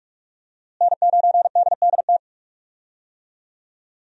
Morse code, heard as a keyed tone, about 46 wpm, 700 Hz. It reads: N9DDT
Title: USA Call Signs - 5 Characters - Repeated 3x - 45wpm